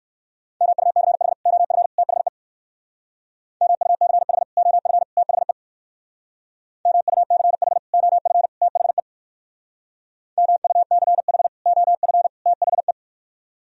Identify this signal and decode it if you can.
Morse code, heard as a keyed tone, about 56 wpm, 700 Hz. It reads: MUCH OF THE MUCH OF THE MUCH OF THE MUCH OF THE